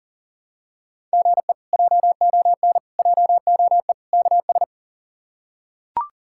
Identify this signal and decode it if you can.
Morse code, heard as a keyed tone, about 40 wpm, 700 Hz. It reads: GE JON JOE KS
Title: K1USN SST QSO Practice - 40wpm